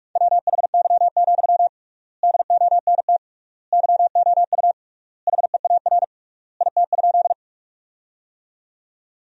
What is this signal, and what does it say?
Morse code, heard as a keyed tone, about 44 wpm, 700 Hz. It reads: WHY, DONT YOU HEAR IT?
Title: 3x Word-Spacing - Sentences from Top 300 Words - 45wpm